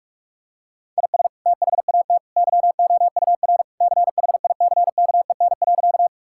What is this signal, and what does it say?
Morse code, heard as a keyed tone, about 41 wpm, 700 Hz. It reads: IS THAT YOUR CHICKEN.
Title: Sentences from Top 1000 Words - Review Part 5 - 45wpm